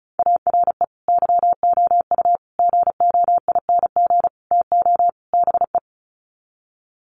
Morse code, heard as a keyed tone, about 34 wpm, 700 Hz. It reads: ARE YOU GOING TO BE
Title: Sets of 5 Words Encore 35wpm